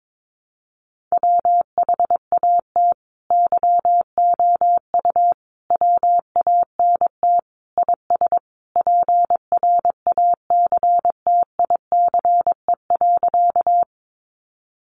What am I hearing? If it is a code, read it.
Morse code, heard as a keyed tone, about 22 wpm, 700 Hz. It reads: WHAT YOU WANT IS PRACTICE.
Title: Sentences from Top 700 Words 22wpm